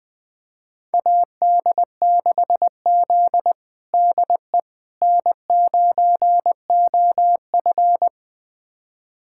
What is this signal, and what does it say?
Morse code, heard as a keyed tone, about 20 wpm, 700 Hz. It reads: AD6Z DE N9OF
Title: USA Callsign DE Callsign - 4 Characters - 20wpm